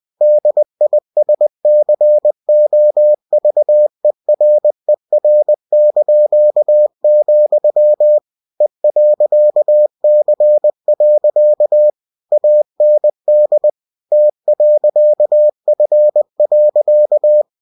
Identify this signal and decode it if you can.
Morse code, heard as a keyed tone, about 20 wpm, 600 Hz. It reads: DISCOVERER), E.C. AND T.F.
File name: cw-books / winnie-the-pooh / winnie-20wpm-page-0099